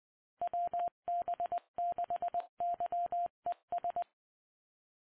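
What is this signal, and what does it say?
Morse code, heard as a keyed tone, about 20 wpm, 700 Hz. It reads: WB6YES
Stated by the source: Morse ident of the PAPA System analog repeaters, WD6FZA.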